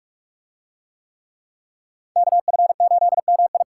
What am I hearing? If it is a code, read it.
Morse code, heard as a keyed tone, about 44 wpm, 700 Hz. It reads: KF8MI